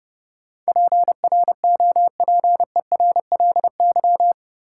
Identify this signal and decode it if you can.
Morse code, heard as a keyed tone, about 30 wpm, 700 Hz. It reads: PROPERLY
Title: Sentences from Top 1100 Words - New Word Drill Alt - 30wpm